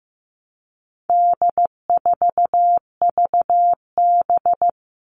Morse code, heard as a keyed tone, about 15 wpm, 700 Hz. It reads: D4VB
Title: All Cut Numbers - 4-Digit 15wpm